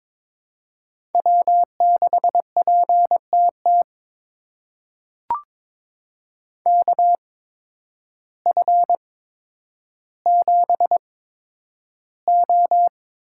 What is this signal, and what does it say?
Morse code, heard as a keyed tone, about 22 wpm, spaced as for 10 wpm, 700 Hz. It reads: W6PTT KF7O
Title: USA Call Signs - 8x Character Spacing - 22wpm